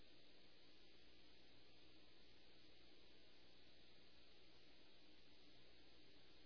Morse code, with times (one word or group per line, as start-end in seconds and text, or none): none